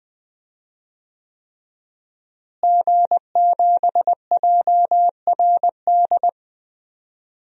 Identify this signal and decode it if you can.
Morse code, heard as a keyed tone, about 20 wpm, 700 Hz. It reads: G7JRD
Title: DXCC-100 Call Signs - 5 Characters - Repeated 3x - 20wpm